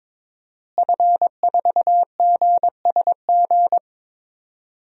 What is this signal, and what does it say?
Morse code, heard as a keyed tone, about 22 wpm, 700 Hz. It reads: F4GSG
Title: DXCC-100 Call Signs - 5 Characters - Repeated 3x - 22wpm